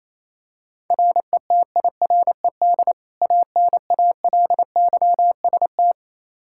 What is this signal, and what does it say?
Morse code, heard as a keyed tone, about 28 wpm, 700 Hz. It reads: RETIRED ANALYST
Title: K1USN SST QSO Practice 28wpm